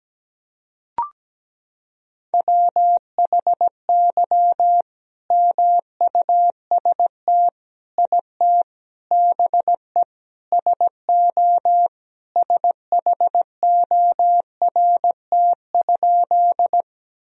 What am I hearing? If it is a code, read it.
Morse code, heard as a keyed tone, about 17 wpm, 700 Hz. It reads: WHY MUST IT BE SO SHORT?
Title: Sentences from Top 1000 Words - Review Part 2 - 17wpm